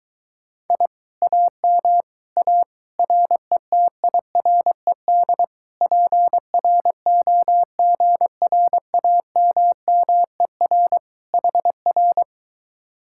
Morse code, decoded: I AM A RETIRED PROGRAMMER HR

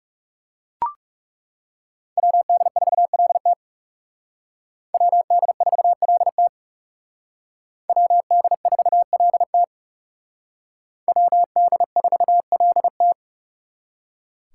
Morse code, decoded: WD4LT WD4LT WD4LT WD4LT

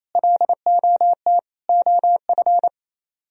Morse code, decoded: LOT OF

